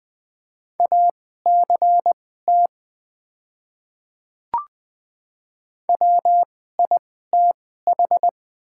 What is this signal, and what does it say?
Morse code, heard as a keyed tone, about 20 wpm, spaced as for 15 wpm, 700 Hz. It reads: ACT WITH
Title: 2x Character Spacing - Top 200 Words - 20wpm